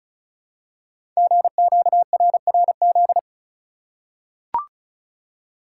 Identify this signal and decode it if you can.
Morse code, heard as a keyed tone, about 34 wpm, 690 Hz. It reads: GQRRZ
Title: Code Groups - 5 Letters - 35wpm